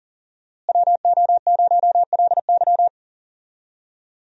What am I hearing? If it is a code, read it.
Morse code, heard as a keyed tone, about 40 wpm, 700 Hz. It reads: WO0LY
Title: USA Call Signs - 5 Characters - Repeated 3x - 40wpm